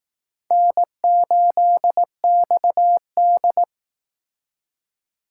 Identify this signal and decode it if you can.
Morse code, heard as a keyed tone, about 18 wpm, 700 Hz. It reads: N8XD
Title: Speed Racing - USA Call Signs 15wpm